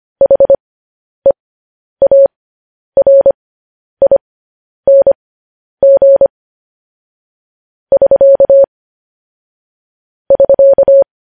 Morse code, heard as a keyed tone, about 25 wpm, spaced as for 10 wpm, 550 Hz. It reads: HEARING <SK> <SK>
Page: news-headlines-in-morse-code-at-10-wpm